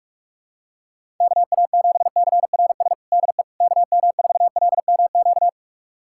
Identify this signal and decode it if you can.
Morse code, heard as a keyed tone, about 46 wpm, 700 Hz. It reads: KA7CRS DE KM4LMQ
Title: USA Callsign DE Callsign 45wpm